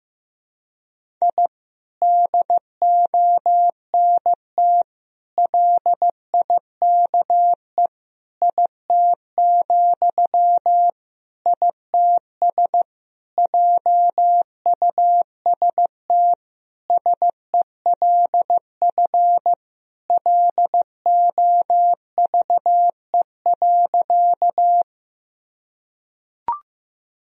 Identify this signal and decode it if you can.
Morse code, heard as a keyed tone, about 15 wpm, 700 Hz. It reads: I DONT LIKE IT, ITS JUST SELF LOVE.
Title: Sentences from Top 1000 Words - Review Part 1 - 15wpm